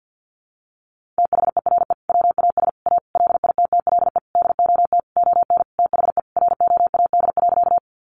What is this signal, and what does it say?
Morse code, heard as a keyed tone, about 50 wpm, 700 Hz. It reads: THERE WAS A LITTLE DOT ON THE ROAD.
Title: Sentences from Top 1500 Words 50wpm